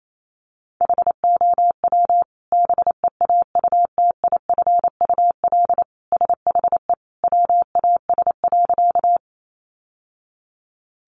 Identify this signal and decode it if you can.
Morse code, heard as a keyed tone, about 28 wpm, 700 Hz. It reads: HOW BEAUTIFUL SHE WAS.
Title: Sentences from Top 900 Words 28wpm